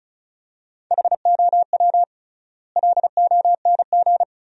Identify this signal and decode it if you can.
Morse code, heard as a keyed tone, about 36 wpm, 700 Hz. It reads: HOW LONG
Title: 3x Word-Spacing - Sentences from Top 500 Words - 35wpm